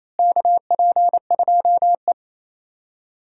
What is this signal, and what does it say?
Morse code, heard as a keyed tone, about 28 wpm, 700 Hz. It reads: KP2E
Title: DXCC-100 Call Signs - 4 Characters - Repeated 3x - 28wpm